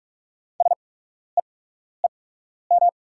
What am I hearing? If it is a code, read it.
Morse code, heard as a keyed tone, about 44 wpm, spaced as for 12 wpm, 690 Hz. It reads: SEEM
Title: 8x Character Spacing - Top 500 Words - 45wpm